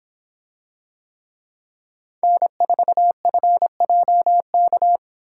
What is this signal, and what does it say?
Morse code, heard as a keyed tone, about 24 wpm, 700 Hz. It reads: N4FJK